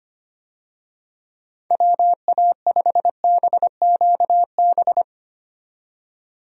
Morse code, heard as a keyed tone, about 25 wpm, 700 Hz. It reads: WA5BQB
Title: USA Call Signs - 6 Characters - 25wpm